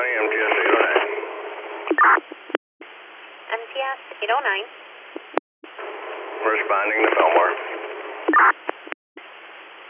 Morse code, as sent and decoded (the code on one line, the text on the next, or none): none